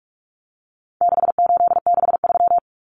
Morse code, heard as a keyed tone, about 44 wpm, 700 Hz. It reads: 6863
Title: Common Cut Numbers - 4-Digit 45wpm